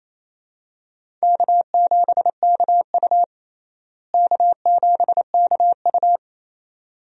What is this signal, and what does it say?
Morse code, heard as a keyed tone, about 28 wpm, 700 Hz. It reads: K7KU K7KU
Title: USA Call Signs - 4 Characters - Repeated 2x - 28wpm